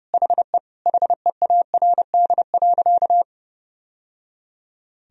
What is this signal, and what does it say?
Morse code, heard as a keyed tone, about 30 wpm, 700 Hz. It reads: HE HEARD.
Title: Sentences from Top 1000 Words - Review Part 1 - 30wpm